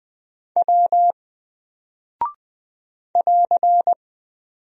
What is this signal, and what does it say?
Morse code, heard as a keyed tone, about 20 wpm, 700 Hz. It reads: W <AR>
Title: Single Character (Letter:Number:Punctuation:Prosign) Alt 20wpm